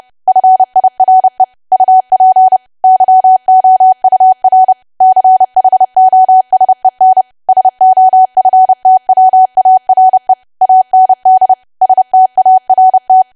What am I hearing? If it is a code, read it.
Morse code, heard as a keyed tone, about 30 wpm, 750 Hz. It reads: FIRE UP YOUR CHOSEN SOFTWARE AND START
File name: cwa / W1AW-Level3 / 30WPM